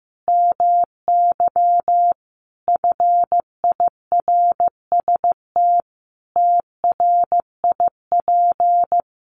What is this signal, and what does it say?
Morse code, heard as a keyed tone, about 15 wpm, 700 Hz. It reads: MY FIRST TRIP
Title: Sentences from Top 1300 Words 15wpm